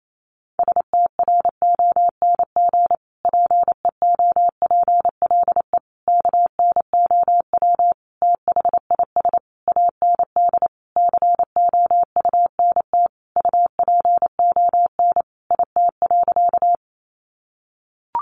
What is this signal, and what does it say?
Morse code, heard as a keyed tone, about 28 wpm, 700 Hz. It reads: STRONG PEOPLE KNOW THIS AND COUNT UPON IT.
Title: Sentences from Top 1500 Words 28wpm